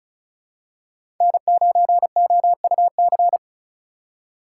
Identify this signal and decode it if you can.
Morse code, heard as a keyed tone, about 35 wpm, 700 Hz. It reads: N9OUC
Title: USA Call Signs - 3x Character Spacing - 35wpm